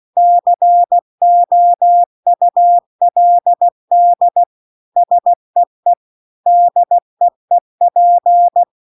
Morse code, heard as a keyed tone, about 16 wpm, 700 Hz. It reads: COULD SEE DEEP
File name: CW-The-Big-Twoheaded-River-2-16WPM